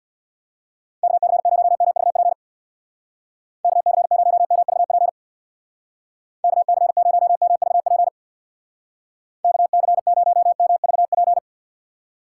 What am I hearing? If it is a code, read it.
Morse code, heard as a keyed tone, about 60 wpm, 700 Hz. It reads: KC0MVP KC0MVP KC0MVP KC0MVP
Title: Speed Racing - USA Call Signs 50wpm